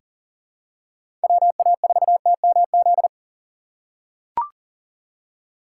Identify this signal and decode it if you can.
Morse code, heard as a keyed tone, about 40 wpm, 700 Hz. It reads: WA4TMZ
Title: USA Call Signs - 6 Characters - 3x Character Spacing - 40wpm